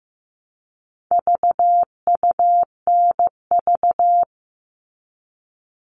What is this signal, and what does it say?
Morse code, heard as a keyed tone, about 15 wpm, 700 Hz. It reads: VUNV
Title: All Cut Numbers - 4-Digit 15wpm